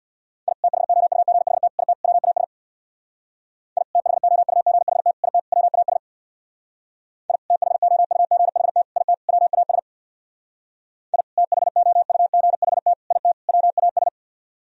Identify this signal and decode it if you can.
Morse code, heard as a keyed tone, about 64 wpm, 700 Hz. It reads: I THOUGHT IT WAS I THOUGHT IT WAS I THOUGHT IT WAS I THOUGHT IT WAS